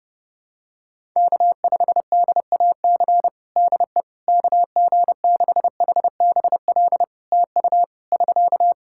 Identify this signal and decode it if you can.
Morse code, heard as a keyed tone, about 30 wpm, 700 Hz. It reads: K5DAC DE KG6HBL TU <SK>
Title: K1USN SST QSO Practice 30wpm